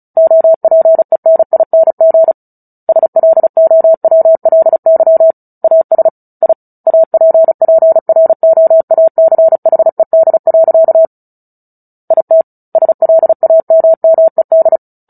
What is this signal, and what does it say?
Morse code, heard as a keyed tone, about 36 wpm, 650 Hz. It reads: OPENING SLOWLY AS I APPROACHED. IT SLAMMED
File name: Book Two / 09-Chapter 9 - 35 WPM